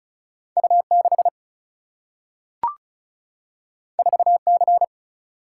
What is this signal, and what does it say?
Morse code, heard as a keyed tone, about 36 wpm, 700 Hz. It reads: U 6 4 C
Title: Single-Letter Single-Number 35wpm